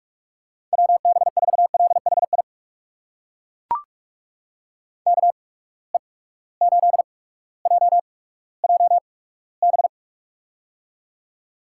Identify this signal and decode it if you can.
Morse code, heard as a keyed tone, about 42 wpm, spaced as for 19 wpm, 700 Hz. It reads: WB4LHI KE8JJB
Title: USA Call Signs - 6 Characters - 8x Character Spacing - 45wpm